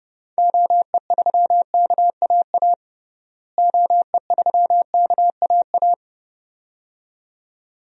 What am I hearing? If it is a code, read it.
Morse code, heard as a keyed tone, about 30 wpm, 700 Hz. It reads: OE3KAA OE3KAA
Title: DXCC-100 Call Signs - 6 Characters - Repeated 2x - 30wpm